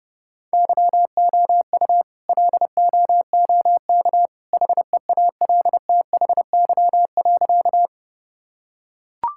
Morse code, heard as a keyed tone, about 30 wpm, 700 Hz. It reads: YOU LOOK HEALTHY.
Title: Sentences from Top 1000 Words - Review Part 5 - 30wpm